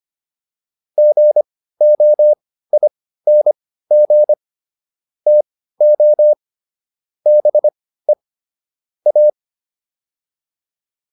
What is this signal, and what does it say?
Morse code, heard as a keyed tone, about 25 wpm, spaced as for 15 wpm, 600 Hz. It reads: GOING TO BE A